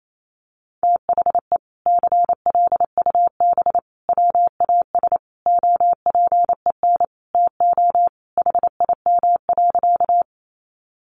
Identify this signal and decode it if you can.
Morse code, heard as a keyed tone, about 28 wpm, 700 Hz. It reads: THE CLUB WAS OPEN TO HIM.
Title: Sentences from Top 1200 Words 28wpm